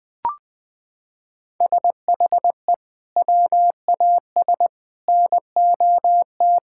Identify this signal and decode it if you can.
Morse code, heard as a keyed tone, about 20 wpm, 700 Hz. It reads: SHE WAS NOT
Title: Sentences with 6-Word Phrases - 20wpm